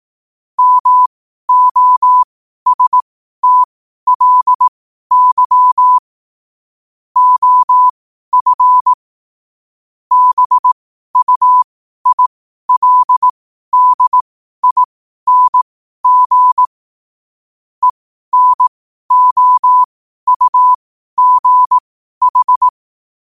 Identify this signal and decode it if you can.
Morse code, heard as a keyed tone, about 18 wpm, spaced as for 12 wpm, 1000 Hz. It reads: MOSTLY OF BUILDING ENOUGH
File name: qotdfiles / 12wpm